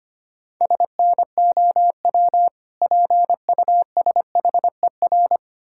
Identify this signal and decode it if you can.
Morse code, heard as a keyed tone, about 26 wpm, 700 Hz. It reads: SNOW PUSHER
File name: practice / 25wpm / Difficult Words - Speed-Racing with Hints - 25wpm